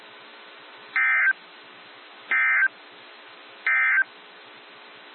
Emergency Alert System (SAME) tones, heard as harsh buzzing bursts. It is heard from 0.8 to 4.1 s.
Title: NOAA Weather Radio - WXJ73